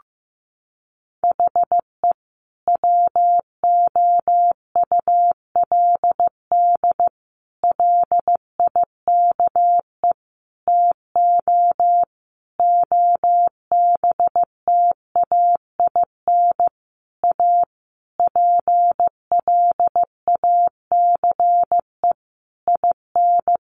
Morse code, heard as a keyed tone, about 15 wpm, 700 Hz. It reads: HE WOULD LIKE TO OBTAIN A PLACE IN
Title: Sentences from Top 1300 Words 15wpm